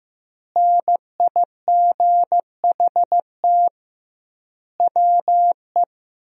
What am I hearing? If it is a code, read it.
Morse code, heard as a keyed tone, about 16 wpm, 700 Hz. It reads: NIGHT WE
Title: 2x Word-Spacing - Sentences from Top 300 Words - 15wpm